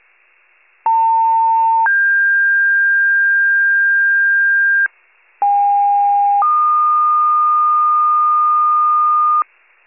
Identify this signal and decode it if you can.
Two-tone page, heard as a one-second tone, then a three-second tone, about 900 then 1600 Hz, about 810 then 1220 Hz, with noise. It is heard from 0.9 to 9.4 s.
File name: wcalert